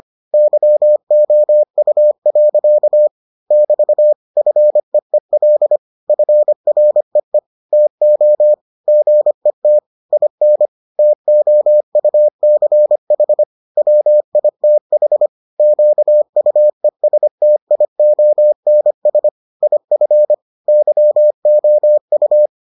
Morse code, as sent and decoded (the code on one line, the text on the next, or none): -.-- --- ..- .-.-.- / -...- / ..-. . .-.. / ..-. .-. . / - --- / --. - / .. -. / - --- ..- -.-. .... / .-- .. - .... / --.- ..- . ... - .. --- -. ... / .. ..-. / -.-- --- ..-
YOU. <BT> FEEL FREE TO GET IN TOUCH WITH QUESTIONS IF YOU